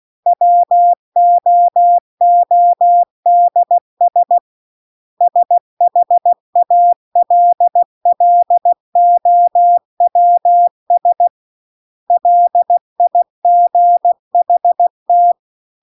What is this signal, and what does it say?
Morse code, heard as a keyed tone, about 16 wpm, 700 Hz. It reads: WOODS SHALLOWS LIGHT
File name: CW-The-Big-Twoheaded-River-2-16WPM